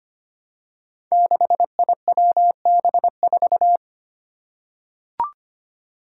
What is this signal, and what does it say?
Morse code, heard as a keyed tone, about 25 wpm, 700 Hz. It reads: 6IWB4
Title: Five Challenging Letter-Number Combinations 25wpm